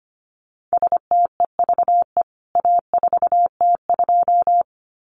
Morse code, heard as a keyed tone, about 25 wpm, 700 Hz. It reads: STEVE A4T2
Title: CWT Exchange 25wpm